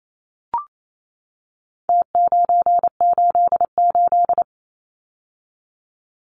Morse code, heard as a keyed tone, about 28 wpm, 700 Hz. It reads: T988
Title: Numbers - Contest Style - 4-Digit 28wpm